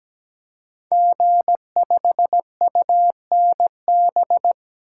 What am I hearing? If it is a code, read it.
Morse code, heard as a keyed tone, about 17 wpm, 700 Hz. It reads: G5UNB
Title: Code Groups - 5 Letter:Numbers - 17wpm